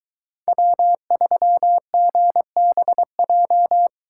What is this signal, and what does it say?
Morse code, heard as a keyed tone, about 22 wpm, 700 Hz. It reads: W3GBJ